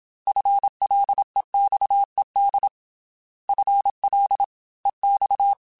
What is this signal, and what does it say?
Morse code, heard as a keyed tone, about 27 wpm, 800 Hz. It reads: FLEXED FL EX
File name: FL / FL mp3s / Fast FL @25